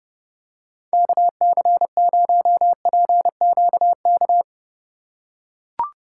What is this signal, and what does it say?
Morse code, heard as a keyed tone, about 29 wpm, 700 Hz. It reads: KC0PQK